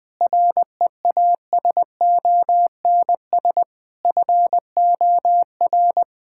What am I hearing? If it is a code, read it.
Morse code, heard as a keyed tone, about 20 wpm, 700 Hz. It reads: REASONS FOR
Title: Sentences with 8-Word Phrases - Spoken Beginning - 20wpm